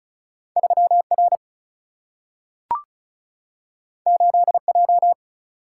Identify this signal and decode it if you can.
Morse code, heard as a keyed tone, about 36 wpm, 700 Hz. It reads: 3 R 8 J
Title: Single-Letter Single-Number 35wpm